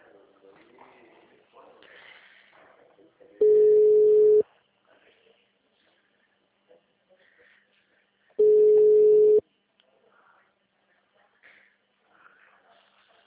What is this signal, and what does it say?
Ringback tone (Europe, 425 Hz), with speech.